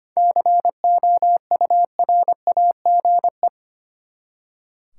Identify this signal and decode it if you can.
Morse code, heard as a keyed tone, about 25 wpm, 700 Hz. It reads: COURAGE
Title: Seven Letter Words - 25wpm